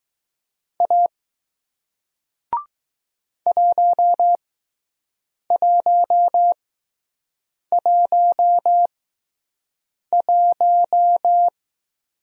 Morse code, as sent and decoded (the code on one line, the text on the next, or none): .- / .---- .---- .---- .----
A 1111